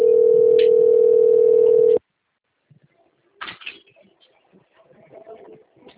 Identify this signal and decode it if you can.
Ringback tone (North America), 440 + 480 Hz.
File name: rg-1600-unknown- - - .24817